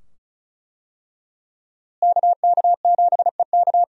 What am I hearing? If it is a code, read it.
Morse code, heard as a keyed tone, about 34 wpm, 700 Hz. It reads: KK7EK
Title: USA Call Signs - 5 Characters - Repeated 2x - 35wpm